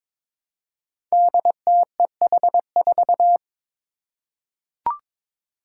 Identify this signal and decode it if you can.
Morse code, heard as a keyed tone, about 22 wpm, 700 Hz. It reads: DTEH4
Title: Five Challenging Letter-Number Combinations 22wpm